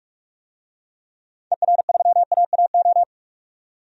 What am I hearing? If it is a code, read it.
Morse code, heard as a keyed tone, about 43 wpm, 700 Hz. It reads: ER3AAO
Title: DXCC-100 Call Signs - 6 Characters - 45wpm